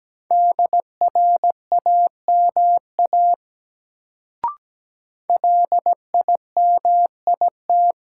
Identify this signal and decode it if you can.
Morse code, heard as a keyed tone, about 17 wpm, 700 Hz. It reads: DRAMA LIMIT
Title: Sentences from Top 1500 Words - New Word Drill Alt - 17wpm